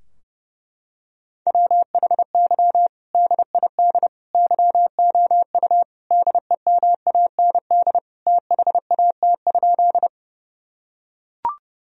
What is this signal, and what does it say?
Morse code, heard as a keyed tone, about 30 wpm, 700 Hz. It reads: WHY DID YOU DEMAND THAT?
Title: Sentences from Top 1000 Words - Review Part 4 - 30wpm